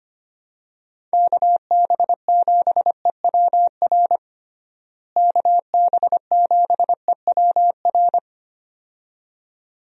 Morse code, heard as a keyed tone, about 25 wpm, 700 Hz. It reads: KB7EWR KB7EWR
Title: USA Call Signs - 6 Characters - Repeated 2x - 25wpm